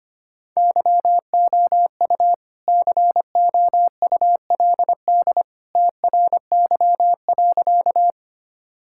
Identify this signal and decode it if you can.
Morse code, heard as a keyed tone, about 25 wpm, 700 Hz. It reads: YOU COULD TRY.
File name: practice / 25wpm / Sentences with 2-Word Phrases - 25wpm